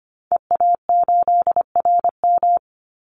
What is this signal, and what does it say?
Morse code, heard as a keyed tone, about 25 wpm, 700 Hz. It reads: EA8RM
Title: Call Signs - 25wpm